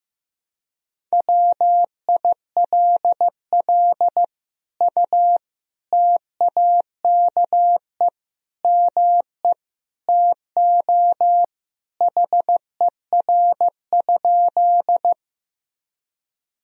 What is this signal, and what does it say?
Morse code, heard as a keyed tone, about 15 wpm, 700 Hz. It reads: WILL U TAKE ME TO HER?
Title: Sentences from Top 200 Words and Abbreviations 15wpm